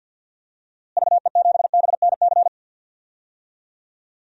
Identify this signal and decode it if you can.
Morse code, heard as a keyed tone, about 48 wpm, 700 Hz. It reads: VE7BNC
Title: DXCC-100 Call Signs - 6 Characters - Repeated 2x - 50wpm